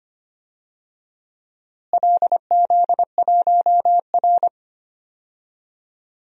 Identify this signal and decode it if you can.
Morse code, heard as a keyed tone, about 25 wpm, 700 Hz. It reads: LZ1R
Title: DXCC-100 Call Signs - 4 Characters - Repeated 3x - 25wpm